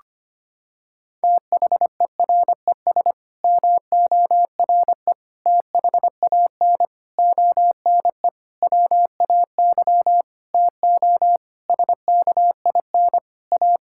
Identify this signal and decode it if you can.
Morse code, heard as a keyed tone, about 25 wpm, 700 Hz. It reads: THERES MORE THAN ONE WAY TO SKIN A
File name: Sets of 9 Words - 25wpm